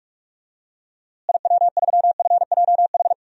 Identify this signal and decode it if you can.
Morse code, heard as a keyed tone, about 46 wpm, 700 Hz. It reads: IW3FJH
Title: DXCC-100 Call Signs - 6 Characters - Repeated 3x - 45wpm